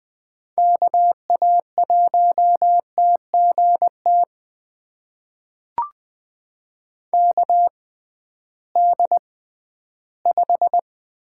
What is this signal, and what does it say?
Morse code, heard as a keyed tone, about 20 wpm, spaced as for 13 wpm, 700 Hz. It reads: KA1TGT KD5
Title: USA Call Signs - 6 Characters - 6x Character Spacing - 20wpm